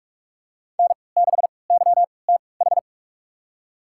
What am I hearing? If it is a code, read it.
Morse code, heard as a keyed tone, about 45 wpm, spaced as for 25 wpm, 700 Hz. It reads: N6YTH
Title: USA Call Signs - 3x Character Spacing - 45wpm